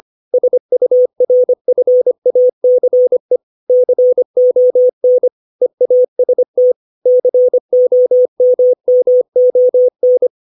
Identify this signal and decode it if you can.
Morse code, heard as a keyed tone, about 25 wpm, 500 Hz. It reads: SURFACE CON EAST COMMON
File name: 632 words / 632 Words 25WPM